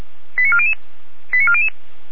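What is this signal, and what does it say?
Five-tone selective call (ZVEI), heard as quick runs of five tones, tones of about 70 ms. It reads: 89400 89400
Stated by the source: ZVEI five-tone alert received by SDR.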